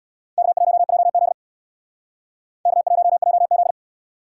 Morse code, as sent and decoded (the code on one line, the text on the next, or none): -.- .---- .--- --.. / -.- .---- .--- --..
K1JZ K1JZ